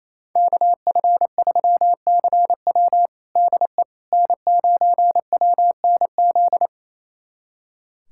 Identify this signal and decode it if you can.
Morse code, heard as a keyed tone, about 28 wpm, 700 Hz. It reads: KF3CW DE N9WNZ
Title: USA Callsign DE Callsign - 5 Characters - 28wpm